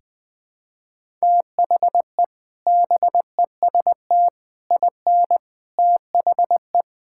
Morse code, heard as a keyed tone, about 20 wpm, 700 Hz. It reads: THE BEST IN THE
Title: Sentences with 3-Word Phrases - 20wpm